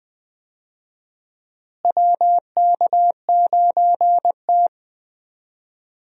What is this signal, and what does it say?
Morse code, heard as a keyed tone, about 20 wpm, 700 Hz. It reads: WK9T
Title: USA Call Signs - 4 Characters - Repeated 2x - 20wpm